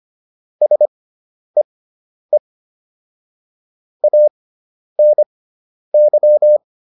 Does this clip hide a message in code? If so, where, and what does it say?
Yes: Morse code from 0.6 to 6.6 s, heard as a keyed tone, about 25 wpm, spaced as for 10 wpm, 600 Hz. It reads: SEE ANY